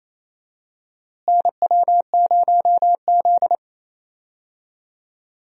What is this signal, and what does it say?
Morse code, heard as a keyed tone, about 28 wpm, 700 Hz. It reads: NW0Z
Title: USA Call Signs - 4 Characters - Repeated 3x - 28wpm